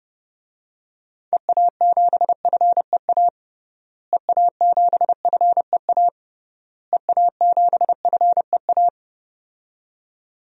Morse code, heard as a keyed tone, about 32 wpm, 700 Hz. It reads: EA7FEA EA7FEA EA7FEA
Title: DXCC-100 Call Signs - 6 Characters - Repeated 3x - 30wpm